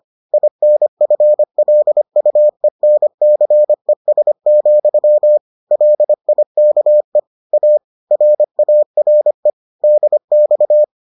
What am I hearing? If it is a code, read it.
Morse code, heard as a keyed tone, about 25 wpm, 600 Hz. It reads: INFLUENCES, LIKE A RARE DX